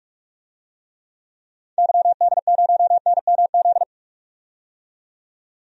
Morse code, heard as a keyed tone, about 46 wpm, 700 Hz. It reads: YD0NMZ
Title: DXCC-100 Call Signs - 6 Characters - 45wpm